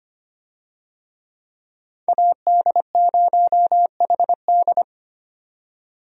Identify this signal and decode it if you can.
Morse code, heard as a keyed tone, about 25 wpm, 700 Hz. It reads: AD0HD
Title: USA Call Signs - 5 Characters - 25wpm